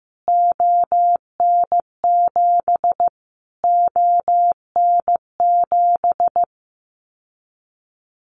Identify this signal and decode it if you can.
Morse code, heard as a keyed tone, about 15 wpm, 700 Hz. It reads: ON7 ON7
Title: Callsign Prefixes for Top 25 DXCC - Repeated 3x - 15wpm